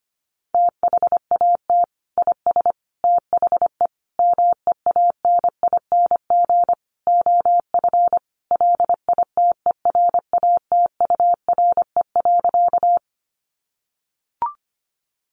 Morse code, heard as a keyed tone, about 25 wpm, 700 Hz. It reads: THAT IS THE MEANING OF LITERATURE.
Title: Sentences from Top 900 Words 25wpm